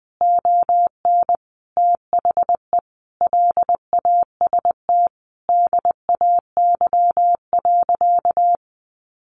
Morse code, heard as a keyed tone, about 20 wpm, 700 Hz. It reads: ON THE LAST DAY.
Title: Sentences from Top 1400 Words 20wpm